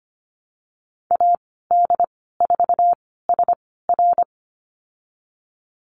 Morse code, heard as a keyed tone, about 25 wpm, spaced as for 16 wpm, 700 Hz. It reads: AD4SR
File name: Lesson 15g - Number 5 - Call Signs 25wpm